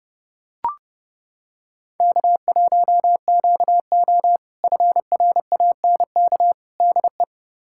Morse code, heard as a keyed tone, about 30 wpm, 700 Hz. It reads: K1QO FRANK DE
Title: K1USN SST QSO Practice - 30wpm